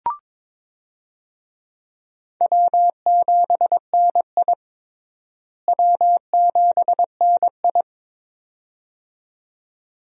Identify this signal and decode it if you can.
Morse code, heard as a keyed tone, about 21 wpm, 700 Hz. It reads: W7NI W7NI
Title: USA Call Signs - 4 Characters - Repeated 2x - 22wpm